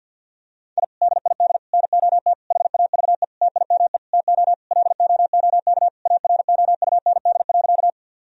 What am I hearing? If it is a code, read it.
Morse code, heard as a keyed tone, about 50 wpm, 700 Hz. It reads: I DID NOT HAVE TIME TO LOOK AROUND.
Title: Sentences from Top 1000 Words - Review Part 3 - 50wpm